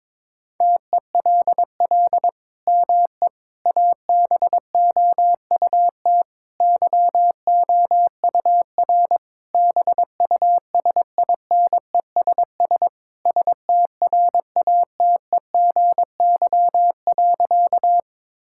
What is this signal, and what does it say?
Morse code, heard as a keyed tone, about 22 wpm, 700 Hz. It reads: TELL ME ABOUT YOUR BUSINESS STRATEGY.
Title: Sentences from Top 1000 Words - Review Part 5 - 22wpm